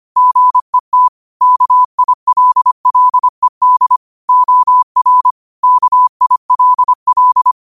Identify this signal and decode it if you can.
Morse code, heard as a keyed tone, about 25 wpm, 1000 Hz. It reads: GET KILLED OR KILL
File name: qotdfiles / 25wpm